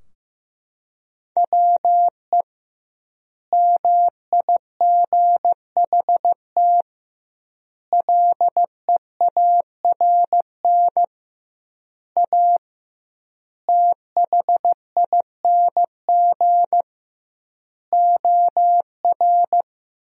Morse code, heard as a keyed tone, about 15 wpm, 700 Hz. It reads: WE MIGHT LEARN A THING OR
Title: 2x Word-Spacing - Sentences from Top 300 Words - 15wpm